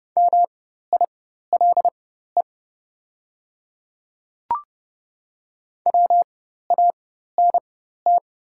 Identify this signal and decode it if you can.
Morse code, heard as a keyed tone, about 30 wpm, spaced as for 14 wpm, 700 Hz. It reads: MILE WANT
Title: 4x Character Spacing - Top 500 Words - 30wpm